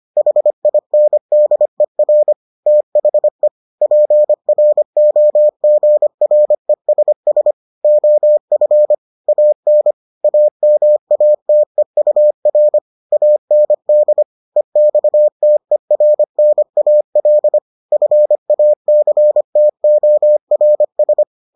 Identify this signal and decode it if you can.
Morse code, heard as a keyed tone, about 25 wpm, 600 Hz. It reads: HINDER THE PROGRESS OF AN AMATEUR AND EXTERNAL FACTORS